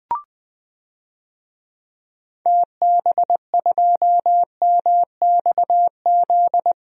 Morse code, heard as a keyed tone, about 20 wpm, 700 Hz. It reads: TB2MXZ
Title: DXCC-100 Call Signs - 6 Characters - Repeated 2x - 20wpm